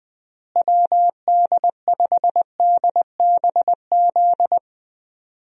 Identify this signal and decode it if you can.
Morse code, heard as a keyed tone, about 21 wpm, 700 Hz. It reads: WD5DBZ
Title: USA Call Signs - 6 Characters - Repeated 2x - 20wpm